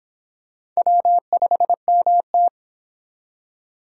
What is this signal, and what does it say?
Morse code, heard as a keyed tone, about 25 wpm, 700 Hz. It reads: W5MT